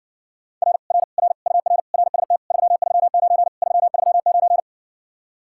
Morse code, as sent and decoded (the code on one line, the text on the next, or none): .-. / .-. / .-. / ..- .-. / .-. ... - / ...-- ...-- ----. / ...-- ...-- ----.
R R R UR RST 339 339